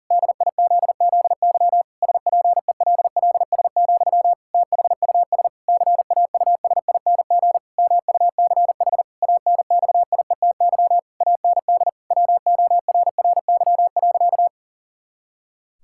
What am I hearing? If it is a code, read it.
Morse code, heard as a keyed tone, about 40 wpm, 700 Hz. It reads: DIZZY SPELLS, THUS CAUSING MUCH ANXIETY AND WORRY.
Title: Pangram Challenge v2 - 40wpm